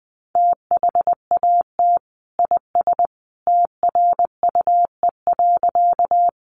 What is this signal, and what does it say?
Morse code, heard as a keyed tone, about 19 wpm, 700 Hz. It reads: THAT IS TRUE.
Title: Sentences from Top 400 Words 20wpm